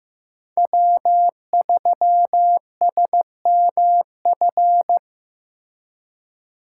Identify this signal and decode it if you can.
Morse code, heard as a keyed tone, about 17 wpm, 700 Hz. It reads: W3SMF
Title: Speed Racing - USA Call Signs 15wpm